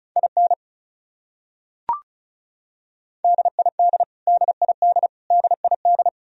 Morse code, heard as a keyed tone, about 34 wpm, 700 Hz. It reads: IN DID DID DID
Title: Top 100 Words Repeated 35wpm